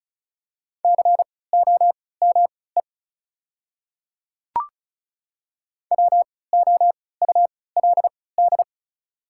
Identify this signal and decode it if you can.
Morse code, heard as a keyed tone, about 36 wpm, spaced as for 20 wpm, 690 Hz. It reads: COME WOULD
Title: 3x Character Spacing - Top 100 Words - 35wpm